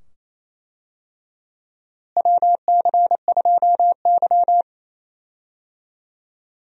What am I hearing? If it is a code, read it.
Morse code, heard as a keyed tone, about 28 wpm, 700 Hz. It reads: WC2Y
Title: USA Call Signs - 4 Characters - Repeated 2x - 28wpm